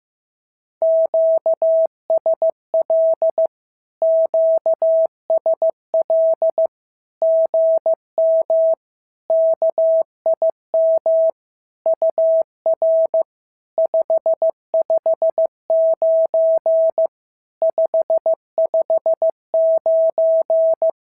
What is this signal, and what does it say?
Morse code, heard as a keyed tone, about 15 wpm, 650 Hz. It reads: QSL QSL GM KIM UR 559 559
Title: SOTA QSO Practice 15wpm